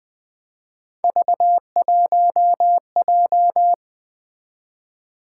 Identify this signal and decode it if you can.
Morse code, heard as a keyed tone, about 24 wpm, 700 Hz. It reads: V1J
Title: Speed-Racing - Three Letter-Number Combinations - 20wpm